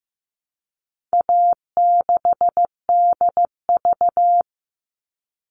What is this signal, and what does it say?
Morse code, heard as a keyed tone, about 15 wpm, 700 Hz. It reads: A6DV
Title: All Cut Numbers - 4-Digit 15wpm